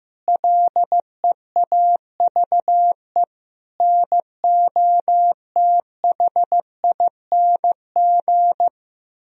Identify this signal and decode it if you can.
Morse code, heard as a keyed tone, about 16 wpm, 700 Hz. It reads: LEAVE NOTHING
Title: Sentences from Top 1000 Words - Review Part 1 - 15wpm